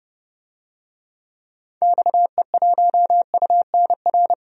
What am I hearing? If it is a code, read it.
Morse code, heard as a keyed tone, about 30 wpm, 700 Hz. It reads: XE1UNR
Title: DXCC-100 Call Signs - 6 Characters - 30wpm